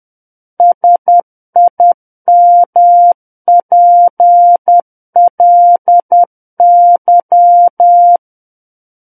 Morse code, heard as a keyed tone, about 10 wpm, 700 Hz. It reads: SIMPLY
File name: Book One / 01-Chapter 1 - 10 WPM